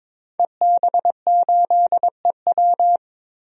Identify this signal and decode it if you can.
Morse code, heard as a keyed tone, about 22 wpm, 700 Hz. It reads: EB8EW
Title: DXCC-100 Call Signs - 5 Characters - 22wpm